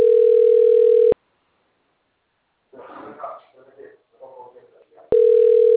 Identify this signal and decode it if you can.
Ringback tone (North America), 440 + 480 Hz, with speech.
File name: rg-1600-unknown- - - .6109